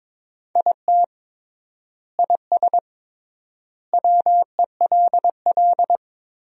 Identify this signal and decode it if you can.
Morse code, heard as a keyed tone, about 22 wpm, 700 Hz. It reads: IT IS WELL
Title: 3x Word-Spacing - Sentences from Top 300 Words - 22wpm